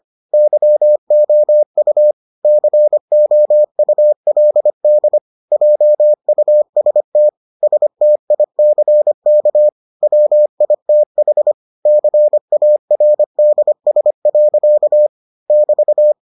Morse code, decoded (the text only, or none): YOU COULD JUST STICK WITH CARDS. <BT>